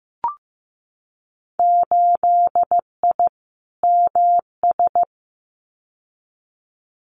Morse code, heard as a keyed tone, about 15 wpm, 700 Hz. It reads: 8I MS
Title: Winter Field Day Exchanges Encore 15wpm